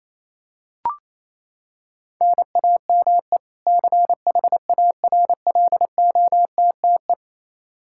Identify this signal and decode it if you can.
Morse code, heard as a keyed tone, about 28 wpm, 700 Hz. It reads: NAME CHARLOTTE
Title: Practice QSO - Exchange 1 - Beginning - 28wpm